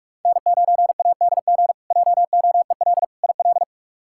Morse code, heard as a keyed tone, about 46 wpm, 700 Hz. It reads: N9ADG JOEL IL
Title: K1USN SST QSO Practice - 45wpm